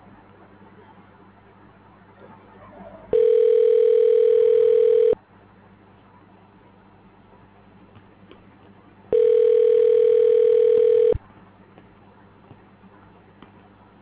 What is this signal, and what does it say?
Ringback tone (North America), 440 + 480 Hz.